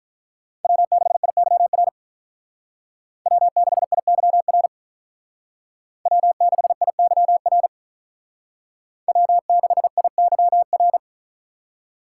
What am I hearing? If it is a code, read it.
Morse code, heard as a keyed tone, about 45 wpm, 700 Hz. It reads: W6IYR W6IYR W6IYR W6IYR